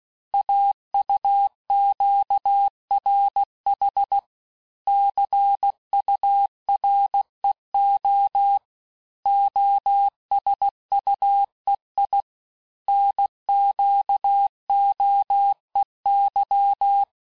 Morse code, heard as a keyed tone, about 16 wpm, 800 Hz. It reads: AUQRH CUREO OSUEI NQOEY
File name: cours-du-lundi files / Cours-1586-texte-chiffre-vitesse-15mpm